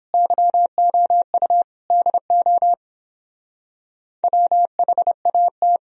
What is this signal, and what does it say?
Morse code, heard as a keyed tone, about 29 wpm, 700 Hz. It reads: YOU DO WHAT